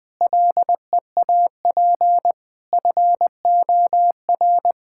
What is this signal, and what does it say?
Morse code, heard as a keyed tone, about 20 wpm, 700 Hz. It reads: LEAP FOR
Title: Difficult Sentences with Hints - 20wpm